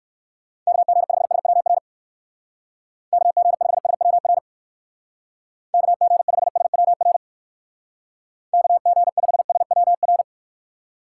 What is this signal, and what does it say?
Morse code, heard as a keyed tone, about 57 wpm, 700 Hz. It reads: KG5SWR KG5SWR KG5SWR KG5SWR